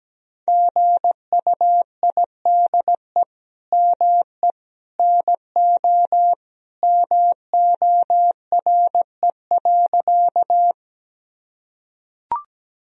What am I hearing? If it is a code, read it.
Morse code, heard as a keyed tone, about 17 wpm, 700 Hz. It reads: GUIDE ME NO MORE.
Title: Sentences from Top 1000 Words - Review Part 3 - 17wpm